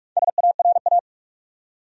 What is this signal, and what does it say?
Morse code, heard as a keyed tone, about 48 wpm, 700 Hz. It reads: SARA
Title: USA Common Names - 45wpm